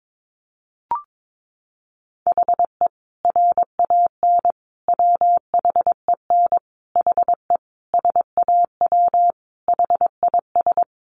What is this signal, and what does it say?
Morse code, heard as a keyed tone, about 22 wpm, 700 Hz. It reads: HE RAN WHEN HE SAW HIS
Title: Sentences from Top 500 Words 22wpm